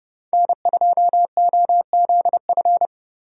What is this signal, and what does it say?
Morse code, heard as a keyed tone, about 31 wpm, 700 Hz. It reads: N2OZF
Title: USA Call Signs - 5 Characters - Repeated 2x - 30wpm